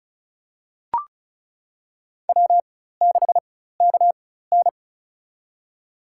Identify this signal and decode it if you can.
Morse code, heard as a keyed tone, about 34 wpm, spaced as for 16 wpm, 700 Hz. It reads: W6KN
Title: USA Call Signs - 4 Characters - 4x Character Spacing - 35wpm